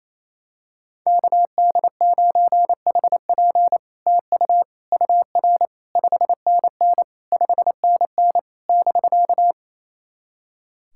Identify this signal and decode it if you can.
Morse code, heard as a keyed tone, about 27 wpm, 700 Hz. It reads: KD9HP TU UR 5NN 5NN <BK>
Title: SOTA QSO Practice 28wpm